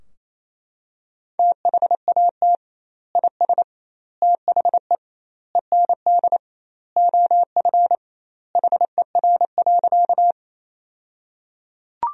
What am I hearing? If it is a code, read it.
Morse code, heard as a keyed tone, about 27 wpm, 700 Hz. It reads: THAT IS THE END OF HER.
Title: 2x Word-Spacing - Sentences from Top 200 Words - 28wpm